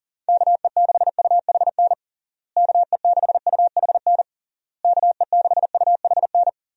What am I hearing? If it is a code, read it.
Morse code, heard as a keyed tone, about 40 wpm, 700 Hz. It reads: KE6UHN KE6UHN KE6UHN
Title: USA Call Signs - 6 Characters - Repeated 3x - 40wpm